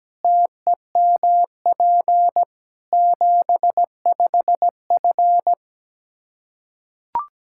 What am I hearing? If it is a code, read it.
Morse code, heard as a keyed tone, about 17 wpm, 700 Hz. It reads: TEMP 75F
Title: Practice QSO - Exchange 2 - Info - 17wpm